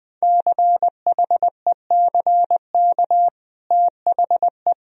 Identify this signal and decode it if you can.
Morse code, heard as a keyed tone, about 21 wpm, 700 Hz. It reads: CHECK THE
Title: Sentences from Top 1000 Words - Review Part 3 - 20wpm